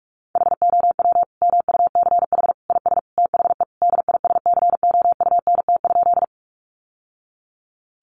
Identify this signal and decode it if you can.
Morse code, heard as a keyed tone, about 44 wpm, 700 Hz. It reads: HOW MUCH IS THE DISCOUNT?
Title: Sentences from Top 1200 Words 45wpm